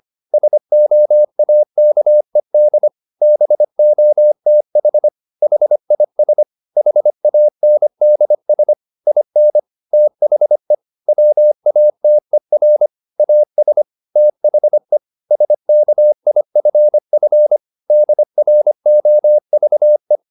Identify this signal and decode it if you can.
Morse code, heard as a keyed tone, about 25 wpm, 600 Hz. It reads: SOAKED BOTH HIS HANDS IN THE WATER AS THE SKIFF DROVE